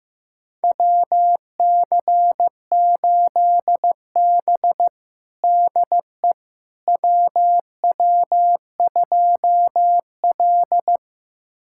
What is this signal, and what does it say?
Morse code, heard as a keyed tone, about 15 wpm, 700 Hz. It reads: WC8B DE WW2L
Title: USA Callsign DE Callsign - 4 Characters - 15wpm